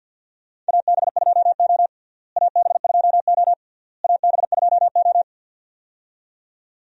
Morse code, heard as a keyed tone, about 52 wpm, 700 Hz. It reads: AB2O AB2O AB2O
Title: USA Call Signs - 4 Characters - Repeated 3x - 50wpm